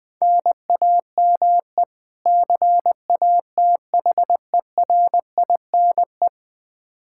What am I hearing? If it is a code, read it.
Morse code, heard as a keyed tone, about 20 wpm, 700 Hz. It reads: NAME CATHERINE
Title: Practice QSO - Exchange 1 - Beginning - 20wpm